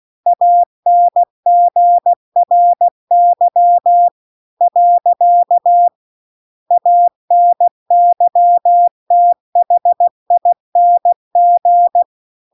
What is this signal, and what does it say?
Morse code, heard as a keyed tone, about 16 wpm, 700 Hz. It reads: ANGRY . ANYTHING
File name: CW-The-Big-Twoheaded-River-2-16WPM